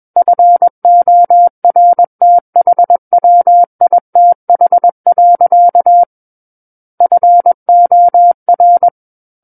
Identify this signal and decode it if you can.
Morse code, heard as a keyed tone, about 21 wpm, 700 Hz. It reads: FORTHWITH. FOR
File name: Book One / 12-Chapter 12 - 21 WPM